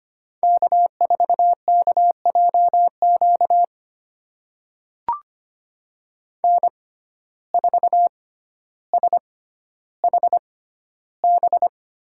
Morse code, heard as a keyed tone, about 25 wpm, spaced as for 13 wpm, 700 Hz. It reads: K4KJQ N4SHB
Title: USA Call Signs - 5 Characters - 6x Character Spacing - 25wpm